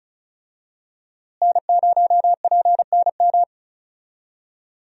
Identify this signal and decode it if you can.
Morse code, heard as a keyed tone, about 36 wpm, 700 Hz. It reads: N0PNM
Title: USA Call Signs - 5 Characters - Repeated 2x - 35wpm